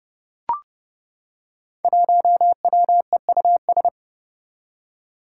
Morse code, heard as a keyed tone, about 30 wpm, 700 Hz. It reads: 1WEUS
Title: Five Challenging Letter-Number Combinations 30wpm